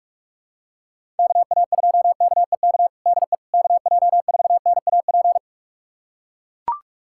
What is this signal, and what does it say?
Morse code, heard as a keyed tone, about 41 wpm, 700 Hz. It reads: KA2KEK DE KJ4NAP
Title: USA Callsign DE Callsign - 6 Characters - 45wpm